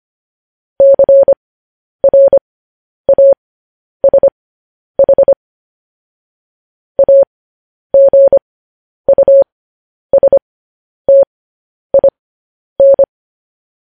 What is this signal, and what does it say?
Morse code, heard as a keyed tone, about 25 wpm, spaced as for 10 wpm, 550 Hz. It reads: CRASH AGUSTIN
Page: news headlines in morse code at 10 wpm